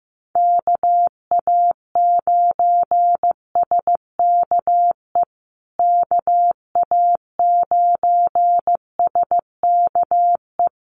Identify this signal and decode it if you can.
Morse code, heard as a keyed tone, about 15 wpm, 700 Hz. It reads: KA9SKE KA9SKE
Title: USA Call Signs Repeated 2x 15wpm